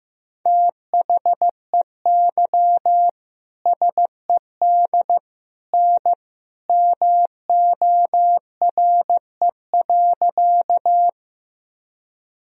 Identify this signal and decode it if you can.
Morse code, heard as a keyed tone, about 15 wpm, 700 Hz. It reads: THEY SED N MORE.
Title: Sentences from Top 100 Words and Abbreviations 15wpm